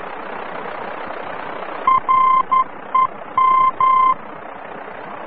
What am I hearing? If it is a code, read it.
Morse code, heard as a keyed tone, about 11 wpm, 1040 Hz. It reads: REM
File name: france / mx / REM-VOR-F